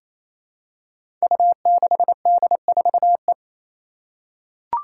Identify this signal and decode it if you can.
Morse code, heard as a keyed tone, about 27 wpm, 700 Hz. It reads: U6D4E
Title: Five Challenging Letter-Number Combinations 28wpm